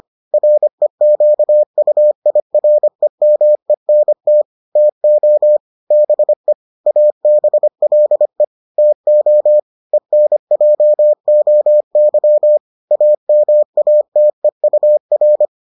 Morse code, heard as a keyed tone, about 25 wpm, 600 Hz. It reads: REQUIREMENT TO BE ABLE TO ENJOY AMATEUR